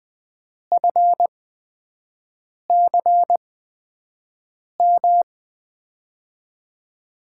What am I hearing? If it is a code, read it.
Morse code, heard as a keyed tone, about 20 wpm, spaced as for 5 wpm, 700 Hz. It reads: FCM
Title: USA Call Signs - 6 Characters - 8x Character Spacing - 20wpm